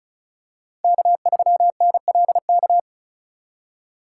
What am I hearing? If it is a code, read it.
Morse code, heard as a keyed tone, about 34 wpm, 690 Hz. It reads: K3NLK
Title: USA Call Signs - 5 Characters - 2x Character Spacing - 35wpm